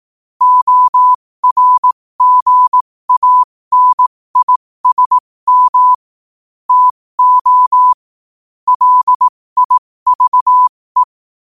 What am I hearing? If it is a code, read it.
Morse code, heard as a keyed tone, about 18 wpm, spaced as for 15 wpm, 1000 Hz. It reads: ORGANISM TO LIVE
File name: qotdfiles / 15wpm